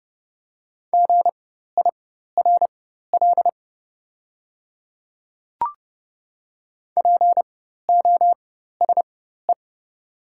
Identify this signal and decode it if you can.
Morse code, heard as a keyed tone, about 30 wpm, spaced as for 14 wpm, 690 Hz. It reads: GIRL POSE
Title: 4x Character Spacing - Top 500 Words - 30wpm